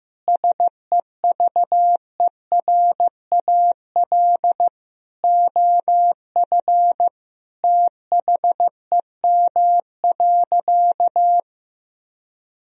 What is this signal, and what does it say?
Morse code, heard as a keyed tone, about 15 wpm, 700 Hz. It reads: SEVERAL OF THEM.
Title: Sentences from Top 1000 Words - Review Part 2 - 15wpm